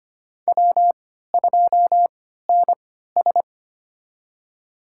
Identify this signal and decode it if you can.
Morse code, heard as a keyed tone, about 25 wpm, spaced as for 14 wpm, 700 Hz. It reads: W2NS
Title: USA Call Signs - 4 Characters - 3x Character Spacing - 25wpm